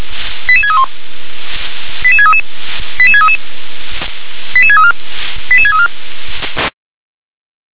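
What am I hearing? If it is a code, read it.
Five-tone selective call (ZVEI), heard as quick runs of five tones, tones of about 70 ms. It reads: 88631 88633 88633 88634 88634
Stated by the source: ZVEI five-tone selective call 88730, received by SDR.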